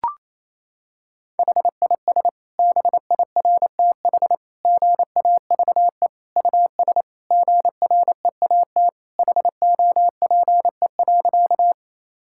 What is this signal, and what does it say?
Morse code, heard as a keyed tone, about 28 wpm, 700 Hz. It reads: HIS BIRTH GAVE US GREAT HOPE.
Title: Sentences from Top 1000 Words - Review Part 5 - 28wpm